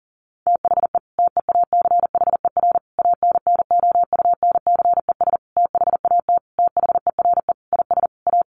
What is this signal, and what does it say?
Morse code, heard as a keyed tone, about 40 wpm, 700 Hz. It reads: THE TEACHER ANNOUNCES THAT THERE IS A
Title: Additional Easy Phrases 40wpm